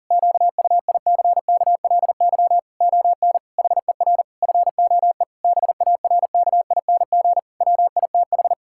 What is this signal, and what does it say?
Morse code, heard as a keyed tone, about 37 wpm, 700 Hz. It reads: QUICKLY ON HER FOE BARKING WITH
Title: Pangram Challenge v2 - 40wpm